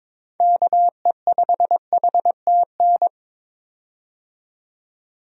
Morse code, heard as a keyed tone, about 22 wpm, 700 Hz. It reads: KE5HTN
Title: USA Call Signs - 6 Characters - 22wpm